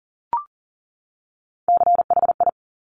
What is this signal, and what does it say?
Morse code, heard as a keyed tone, about 42 wpm, 690 Hz. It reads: CHI
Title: Top Three Letter Combinations in Words 40wpm